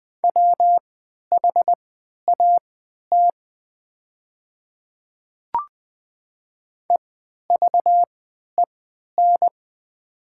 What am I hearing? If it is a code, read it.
Morse code, heard as a keyed tone, about 20 wpm, spaced as for 11 wpm, 700 Hz. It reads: WHAT EVEN
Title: 3x Character Spacing - Top 200 Words - 20wpm